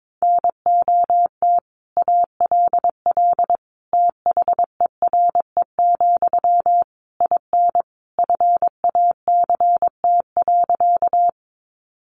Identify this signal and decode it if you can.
Morse code, heard as a keyed tone, about 22 wpm, 700 Hz. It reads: NOT ALL THERE, IN FACT.
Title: Sentences from Top 600 Words 22wpm